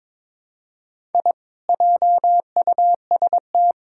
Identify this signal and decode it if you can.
Morse code, heard as a keyed tone, about 21 wpm, 700 Hz. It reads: I JUST
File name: practice / 22wpm / Sentences with 4-Word Phrases - 22wpm